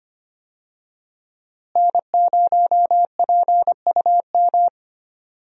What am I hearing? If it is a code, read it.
Morse code, heard as a keyed tone, about 25 wpm, 700 Hz. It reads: N0PUM
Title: USA Call Signs - 5 Characters - Repeated 2x - 25wpm